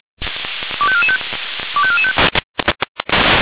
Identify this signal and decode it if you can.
Five-tone selective call (ZVEI), heard as quick runs of five tones, tones of about 70 ms. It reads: 25445 25445
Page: SDR/ZVEI